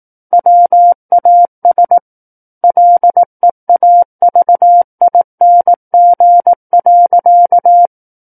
Morse code, heard as a keyed tone, about 18 wpm, 700 Hz. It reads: WAS LEAVING.
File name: 18 WPM